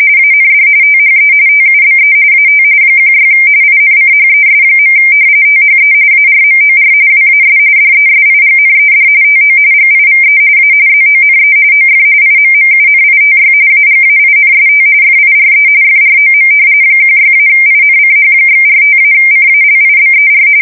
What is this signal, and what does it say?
RTTY, about 45.45 baud. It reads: AND THE TUBE AT 16 INCHES. THE ARMOR WAS DESIGNED TO GIVE A ZONE OF IMMUNITY AGAINST FIRE FROM 16-INCH/45-CALIBER
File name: fsk170 / battleshipnj